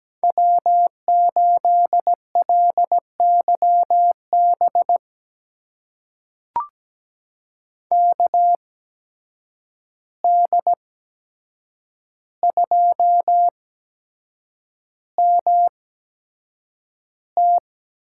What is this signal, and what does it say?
Morse code, heard as a keyed tone, about 17 wpm, spaced as for 7 wpm, 700 Hz. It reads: W8LYB KD2MT
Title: USA Call Signs - 8x Character Spacing - 17wpm